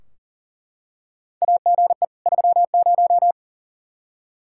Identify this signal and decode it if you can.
Morse code, heard as a keyed tone, about 40 wpm, 700 Hz. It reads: AGE 30
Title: Practice QSO - Exchange 3 - More Info - 40wpm